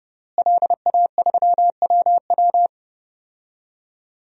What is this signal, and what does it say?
Morse code, heard as a keyed tone, about 30 wpm, 700 Hz. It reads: LA3WW
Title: DXCC-100 Call Signs - 5 Characters - Repeated 2x - 30wpm